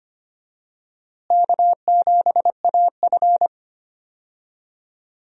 Morse code, heard as a keyed tone, about 25 wpm, 700 Hz. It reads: K7AF
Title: USA Call Signs - 4 Characters - Repeated 3x - 25wpm